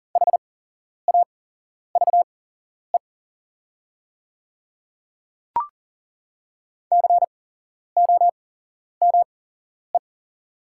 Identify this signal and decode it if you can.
Morse code, heard as a keyed tone, about 40 wpm, spaced as for 11 wpm, 690 Hz. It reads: HAVE COME
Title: 8x Character Spacing - Top 200 Words - 40wpm